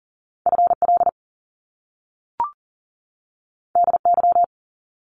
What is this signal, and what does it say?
Morse code, heard as a keyed tone, about 42 wpm, 700 Hz. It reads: FL DY
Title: Top Two Letter Combinations in Words 40wpm